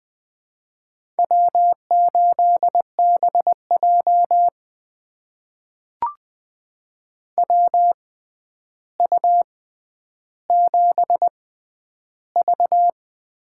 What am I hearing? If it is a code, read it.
Morse code, heard as a keyed tone, about 20 wpm, spaced as for 10 wpm, 700 Hz. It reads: W8BJ WU7V
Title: USA Call Signs - 6x Character Spacing - 20wpm